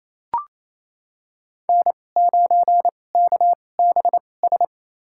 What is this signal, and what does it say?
Morse code, heard as a keyed tone, about 28 wpm, spaced as for 20 wpm, 700 Hz. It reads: N9KBS
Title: USA Call Signs - 5 Characters - 2x Character Spacing - 28wpm